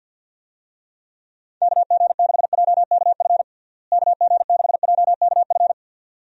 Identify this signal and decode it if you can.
Morse code, heard as a keyed tone, about 53 wpm, 700 Hz. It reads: KG6JKF KG6JKF
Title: USA Call Signs - 6 Characters - Repeated 2x - 50wpm